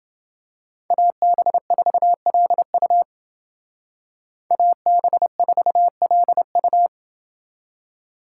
Morse code, heard as a keyed tone, about 29 wpm, 700 Hz. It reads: AB4LU AB4LU